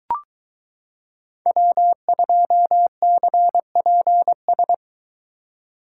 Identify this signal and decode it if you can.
Morse code, heard as a keyed tone, about 22 wpm, 700 Hz. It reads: W2CPS